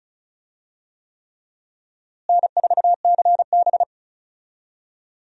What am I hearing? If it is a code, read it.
Morse code, heard as a keyed tone, about 34 wpm, 700 Hz. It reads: N4CB
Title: USA Call Signs - 4 Characters - Repeated 3x - 35wpm